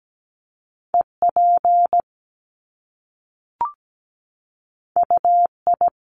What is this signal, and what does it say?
Morse code, heard as a keyed tone, about 17 wpm, 700 Hz. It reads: EP UI
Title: Top Two Letter Combinations in Words 17wpm